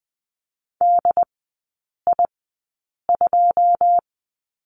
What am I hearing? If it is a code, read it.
Morse code, heard as a keyed tone, about 20 wpm, spaced as for 9 wpm, 690 Hz. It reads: DI2
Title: 093 Lesson 16e - Letter c - Three Characters 20wpm